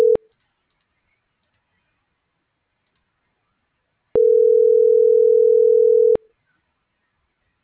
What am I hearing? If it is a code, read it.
Ringback tone (North America), 440 + 480 Hz.